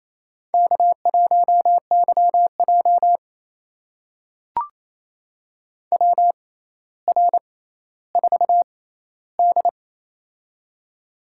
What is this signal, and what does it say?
Morse code, heard as a keyed tone, about 27 wpm, spaced as for 14 wpm, 700 Hz. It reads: K1YJ WR4D
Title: USA Call Signs - 4 Characters - 6x Character Spacing - 28wpm